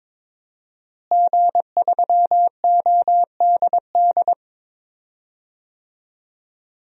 Morse code, heard as a keyed tone, about 22 wpm, 700 Hz. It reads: G3ODD
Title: DXCC-100 Call Signs - 5 Characters - Repeated 3x - 22wpm